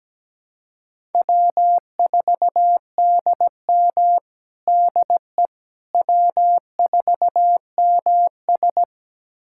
Morse code, heard as a keyed tone, about 17 wpm, 700 Hz. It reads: W4DM DE W4MS
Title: USA Callsign DE Callsign - 4 Characters - 17wpm